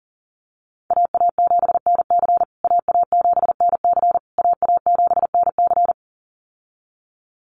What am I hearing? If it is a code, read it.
Morse code, heard as a keyed tone, about 40 wpm, 700 Hz. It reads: AA7NC AA7NC AA7NC
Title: USA Call Signs Repeated 3x 40wpm